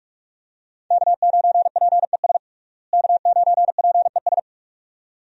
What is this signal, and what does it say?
Morse code, heard as a keyed tone, about 45 wpm, 700 Hz. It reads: K9PES K9PES
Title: USA Call Signs - 5 Characters - Repeated 2x - 45wpm